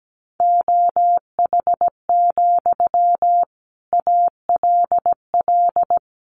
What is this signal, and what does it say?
Morse code, heard as a keyed tone, about 17 wpm, 700 Hz. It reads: OH, ALL
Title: Sentences from Top 500 Words 17wpm